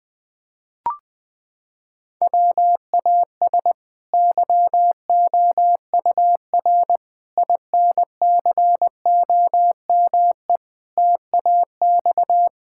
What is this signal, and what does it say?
Morse code, heard as a keyed tone, about 21 wpm, 700 Hz. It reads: WAS YOUR INCOME TAX
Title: Sentences from Top 1000 Words - Review Part 5 - 20wpm